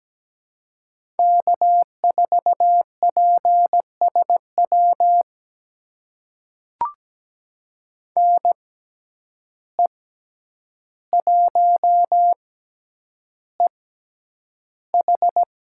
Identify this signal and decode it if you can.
Morse code, heard as a keyed tone, about 17 wpm, spaced as for 9 wpm, 700 Hz. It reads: K4PSW NE1EH
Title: USA Call Signs - 5 Characters - 6x Character Spacing - 17wpm